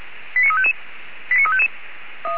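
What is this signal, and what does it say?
Five-tone selective call (ZVEI), heard as quick runs of five tones, tones of about 70 ms. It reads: 89355
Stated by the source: Five-tone ZVEI selective call received on an SDR receiver.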